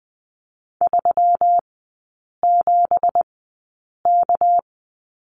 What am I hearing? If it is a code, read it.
Morse code, heard as a keyed tone, about 20 wpm, spaced as for 8 wpm, 700 Hz. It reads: 37K
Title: 198 Lesson 32e - Letter k - Three Characters 20wpm